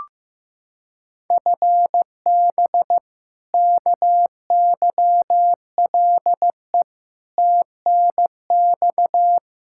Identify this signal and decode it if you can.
Morse code, heard as a keyed tone, about 15 wpm, 700 Hz. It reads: FB KYLE TNX
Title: Practice QSO - Exchange 3 - More Info - 15wpm